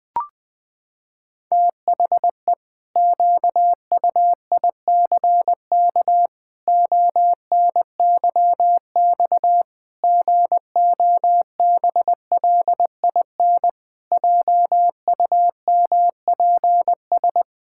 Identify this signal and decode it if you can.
Morse code, heard as a keyed tone, about 20 wpm, 700 Hz. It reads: THE QUICK ONYX GOBLIN JUMPS
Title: Pangram Challenge v2 - 20wpm